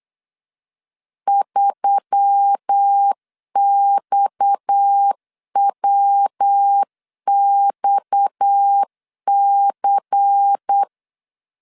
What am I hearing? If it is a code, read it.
Morse code, heard as a keyed tone, about 8 wpm, 800 Hz. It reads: 3XWXC
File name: LEZIONI / 6 PKX0 / 6 PKX 8wpm 3041